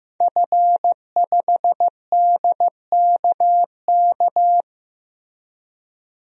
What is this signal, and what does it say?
Morse code, heard as a keyed tone, about 15 wpm, 700 Hz. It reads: F5DKK
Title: DXCC-100 Call Signs - 5 Characters - Repeated 3x - 15wpm